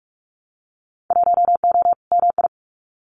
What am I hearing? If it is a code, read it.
Morse code, heard as a keyed tone, about 44 wpm, 700 Hz. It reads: 1O MI
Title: Winter Field Day Exchanges 45wpm